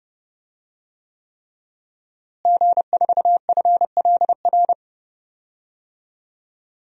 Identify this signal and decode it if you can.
Morse code, heard as a keyed tone, about 31 wpm, 700 Hz. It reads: G4FLR
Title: DXCC-100 Call Signs - 5 Characters - Repeated 3x - 30wpm